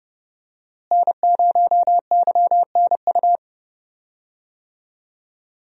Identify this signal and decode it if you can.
Morse code, heard as a keyed tone, about 30 wpm, 700 Hz. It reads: N0YNU
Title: USA Call Signs - 5 Characters - Repeated 2x - 30wpm